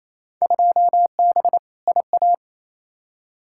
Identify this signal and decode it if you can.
Morse code, heard as a keyed tone, about 28 wpm, 700 Hz. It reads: 2B IA
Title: Field Day Exchanges Encore - 28wpm